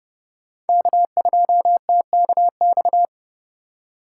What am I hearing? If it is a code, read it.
Morse code, heard as a keyed tone, about 31 wpm, 700 Hz. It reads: K2TKX
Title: USA Call Signs - 5 Characters - 8x Character Spacing - 30wpm